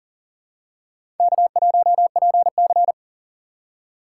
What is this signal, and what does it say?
Morse code, heard as a keyed tone, about 40 wpm, 700 Hz. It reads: K1PC
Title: USA Call Signs - 4 Characters - 6x Character Spacing - 40wpm